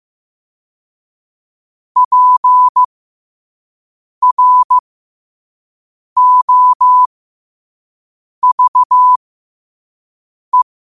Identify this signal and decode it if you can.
Morse code, heard as a keyed tone, about 15 wpm, spaced as for 5 wpm, 1000 Hz. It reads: PROVE
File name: qotdfiles / 5wpm